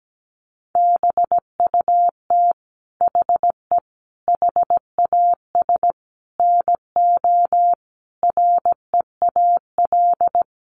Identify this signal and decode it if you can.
Morse code, heard as a keyed tone, about 17 wpm, 700 Hz. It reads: BUT HE HAS NO REAL
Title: Sentences from Top 1300 Words 17wpm